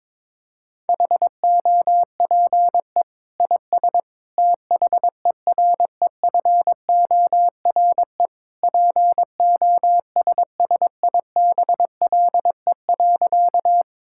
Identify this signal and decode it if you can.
Morse code, heard as a keyed tone, about 22 wpm, 700 Hz. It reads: HOPE IS THEREFORE POSSIBLE.
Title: Sentences from Top 1000 Words - Review Part 5 - 22wpm